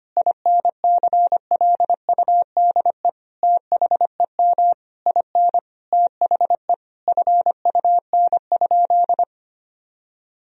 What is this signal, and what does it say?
Morse code, heard as a keyed tone, about 25 wpm, 700 Hz. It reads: INCLUDE THEM IN THE FUN?
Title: Sentences from Top 1000 Words - Review Part 3 - 25wpm